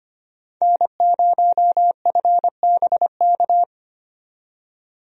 Morse code, heard as a keyed tone, about 25 wpm, 700 Hz. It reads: N0FBK
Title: USA Call Signs - 5 Characters - 25wpm